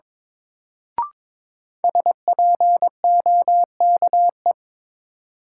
Morse code, heard as a keyed tone, about 22 wpm, 700 Hz. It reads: SPOKE
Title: Sentences from Top 1200 Words - New Word Drill - 22wpm